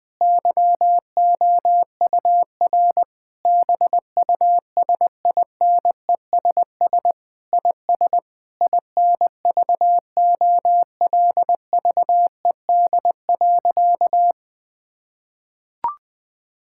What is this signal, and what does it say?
Morse code, heard as a keyed tone, about 21 wpm, 700 Hz. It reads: YOUR BUSINESS IS INVOLVED.
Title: Sentences from Top 1000 Words - Review Part 5 - 20wpm